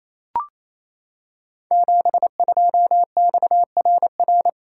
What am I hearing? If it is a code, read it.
Morse code, heard as a keyed tone, about 28 wpm, 700 Hz. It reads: 72XRR
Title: Code Groups - 5 Letter:Numbers - 28wpm